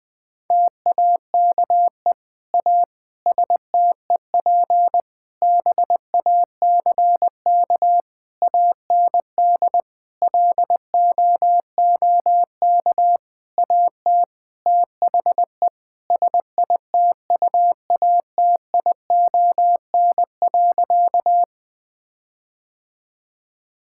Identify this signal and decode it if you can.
Morse code, heard as a keyed tone, about 20 wpm, 700 Hz. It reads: TAKE A STEP BACK AND LOOK AT THE SITUATION.
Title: Sentences with 8-Word Phrases - Spoken Beginning - 20wpm